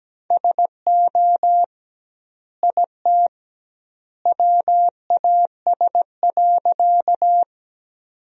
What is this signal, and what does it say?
Morse code, heard as a keyed tone, about 17 wpm, 700 Hz. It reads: SO IT WAS.
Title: 2x Word-Spacing - Sentences from Top 100 Words - 17wpm